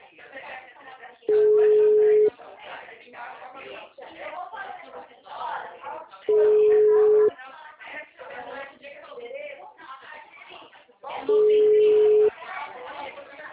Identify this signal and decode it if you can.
Ringback tone (Europe, 425 Hz), with speech.